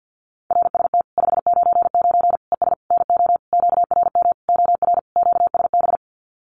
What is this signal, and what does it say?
Morse code, heard as a keyed tone, about 50 wpm, 700 Hz. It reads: RST 599 ES NO QRM OR QSB
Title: Common QSO Phrases 50wpm